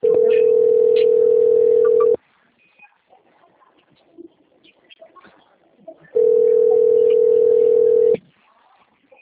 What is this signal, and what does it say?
Ringback tone (North America), 440 + 480 Hz.